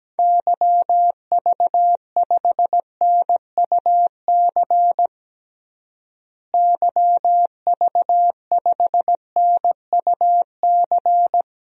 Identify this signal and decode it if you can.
Morse code, heard as a keyed tone, about 17 wpm, 700 Hz. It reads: YV5NUC YV5NUC
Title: DXCC-100 Call Signs - 6 Characters - Repeated 2x - 17wpm